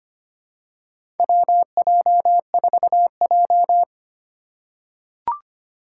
Morse code, heard as a keyed tone, about 25 wpm, 700 Hz. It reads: WJ4J
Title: USA Call Signs - 2x Character Spacing - 25wpm